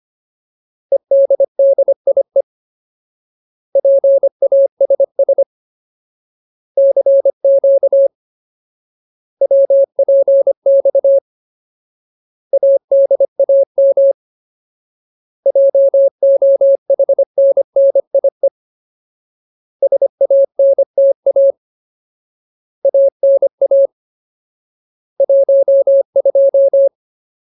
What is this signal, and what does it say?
Morse code, heard as a keyed tone, about 25 wpm, 550 Hz. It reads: EDDIE PASS CQ WPX ADAM JOHNNIE SANTA ANA 12
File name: Ultimate Word List / 25WPM Ultimate